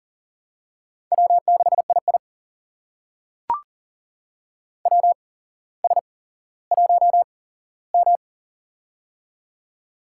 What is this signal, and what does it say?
Morse code, heard as a keyed tone, about 40 wpm, spaced as for 17 wpm, 700 Hz. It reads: W6II WS1M
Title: USA Call Signs - 4 Characters - 8x Character Spacing - 40wpm